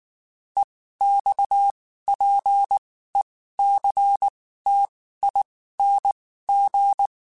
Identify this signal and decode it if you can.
Morse code, heard as a keyed tone, about 19 wpm, spaced as for 14 wpm, 800 Hz. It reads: EXPECTING
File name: (spaced) / QSOs Spaced-mp3s / WA7QOX 1 @18 sp